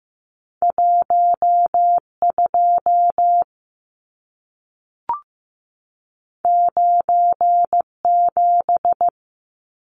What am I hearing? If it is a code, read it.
Morse code, heard as a keyed tone, about 15 wpm, 700 Hz. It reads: 12 97
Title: Numbers - 2-Digit 15wpm